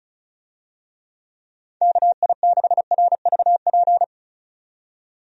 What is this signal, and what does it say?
Morse code, heard as a keyed tone, about 36 wpm, 700 Hz. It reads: KI6RVP
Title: USA Call Signs - 6 Characters - 35wpm